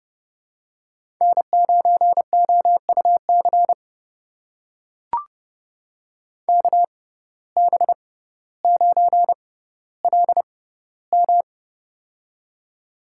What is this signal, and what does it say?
Morse code, heard as a keyed tone, about 30 wpm, spaced as for 15 wpm, 700 Hz. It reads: N9OUC KB9LM
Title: USA Call Signs - 5 Characters - 6x Character Spacing - 30wpm